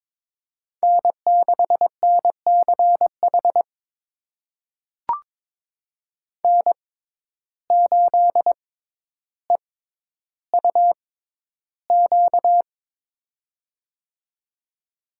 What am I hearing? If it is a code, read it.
Morse code, heard as a keyed tone, about 22 wpm, spaced as for 11 wpm, 700 Hz. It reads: N6NCH N8EUQ
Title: USA Call Signs - 5 Characters - 6x Character Spacing - 22wpm